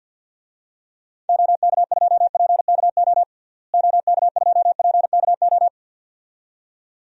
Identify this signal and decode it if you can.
Morse code, heard as a keyed tone, about 50 wpm, 700 Hz. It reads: OK2PKO OK2PKO
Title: DXCC-100 Call Signs - 6 Characters - Repeated 2x - 50wpm